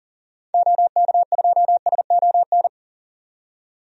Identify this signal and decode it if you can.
Morse code, heard as a keyed tone, about 40 wpm, 700 Hz. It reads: OK2SON
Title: DXCC-100 Call Signs - 6 Characters - 40wpm